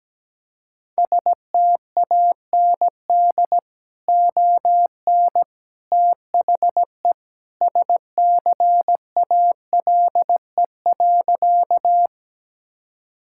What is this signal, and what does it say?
Morse code, heard as a keyed tone, about 17 wpm, 700 Hz. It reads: STAND ON THE SCALE.
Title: Sentences from Top 1000 Words - Review Part 5 - 17wpm